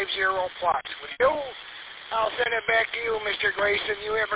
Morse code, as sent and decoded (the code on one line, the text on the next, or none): none